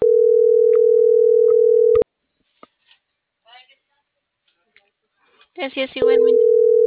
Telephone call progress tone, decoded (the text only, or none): ringback tone